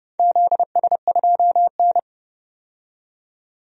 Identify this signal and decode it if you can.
Morse code, heard as a keyed tone, about 30 wpm, 700 Hz. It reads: ZS2N
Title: DXCC-100 Call Signs - 4 Characters - 30wpm